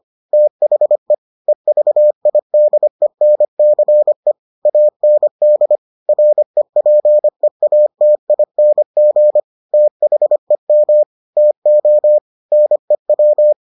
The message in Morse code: THE EVIDENCE AND REPEATING THEM TO NEW

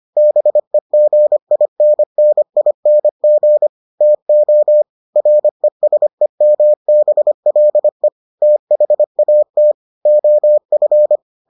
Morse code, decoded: BEGINNING TO RESEMBLE THAT OF

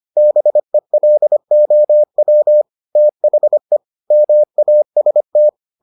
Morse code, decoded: BELOW THE MAST

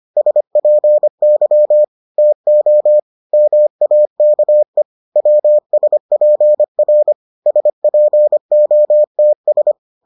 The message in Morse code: SPY TO MAKE WSPR SPOTS

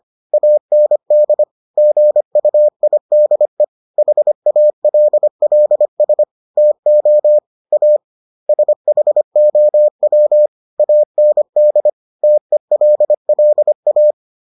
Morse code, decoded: AND GUIDE HALLS TO A SHOW AND TELLA

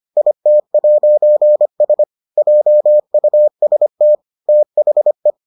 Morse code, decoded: IT'S JUST THE

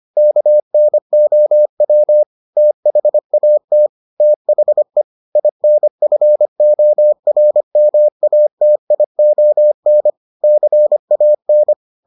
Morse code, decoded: KNOW THAT THE INFORMATION CAN

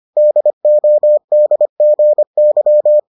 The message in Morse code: DODGY

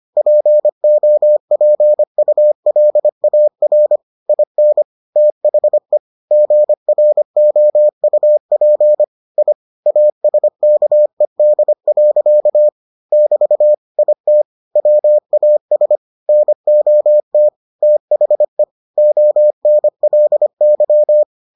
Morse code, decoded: POPULAR IN THE GROUP I ASKED. <BT> IT WAS NOT THE ONLY

